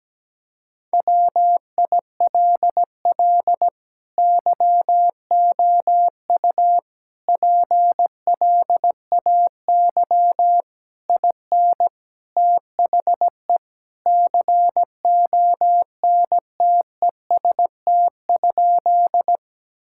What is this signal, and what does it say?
Morse code, heard as a keyed tone, about 17 wpm, 700 Hz. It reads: WILL YOU PLAY IN THE CONTEST?
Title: Sentences from Top 1600 Words 17wpm